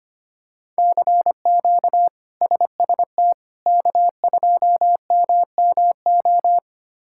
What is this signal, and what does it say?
Morse code, heard as a keyed tone, about 25 wpm, 700 Hz. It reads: CQ SST K2MMO
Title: K1USN SST QSO Practice - 25wpm